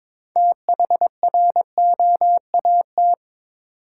Morse code, heard as a keyed tone, about 23 wpm, 700 Hz. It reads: THROAT
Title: Six Letter Words - 22wpm